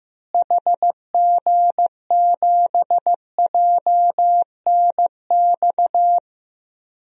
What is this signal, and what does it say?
Morse code, heard as a keyed tone, about 15 wpm, 700 Hz. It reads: HG7JNX
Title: DXCC-100 Call Signs - 6 Characters - Repeated 3x - 15wpm